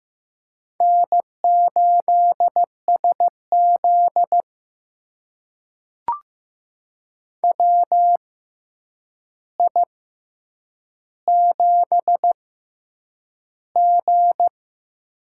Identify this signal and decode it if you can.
Morse code, heard as a keyed tone, about 15 wpm, spaced as for 8 wpm, 700 Hz. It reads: N8SZ WI7G
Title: USA Call Signs - 4 Characters - 6x Character Spacing - 15wpm